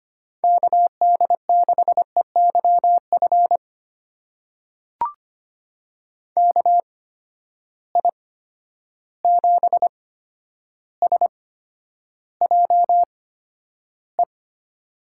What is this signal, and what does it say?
Morse code, heard as a keyed tone, about 25 wpm, spaced as for 11 wpm, 700 Hz. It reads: KD6EYF KI7SJE
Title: USA Call Signs - 6 Characters - 8x Character Spacing - 25wpm